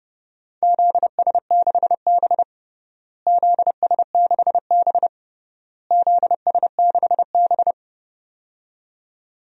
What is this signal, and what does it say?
Morse code, heard as a keyed tone, about 30 wpm, 700 Hz. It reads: ZS6B ZS6B ZS6B
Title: DXCC-100 Call Signs - 4 Characters - Repeated 3x - 30wpm